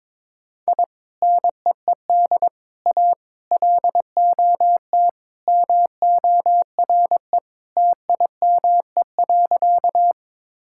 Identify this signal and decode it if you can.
Morse code, heard as a keyed tone, about 22 wpm, 700 Hz. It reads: I NEED A LOT MORE TIME.
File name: practice / 22wpm / Sentences with 3-Word Phrases - 22wpm